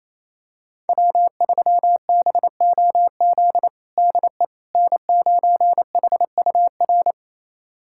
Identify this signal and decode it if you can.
Morse code, heard as a keyed tone, about 28 wpm, 700 Hz. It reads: W3BOZ DE N9HUR
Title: USA Callsign DE Callsign - 5 Characters - 28wpm